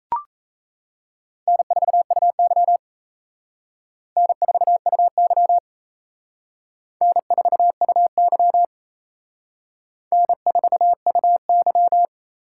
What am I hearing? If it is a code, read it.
Morse code, heard as a keyed tone, about 35 wpm, 700 Hz. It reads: N4UY N4UY N4UY N4UY
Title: Speed Racing - USA 1x2 Call Signs 28wpm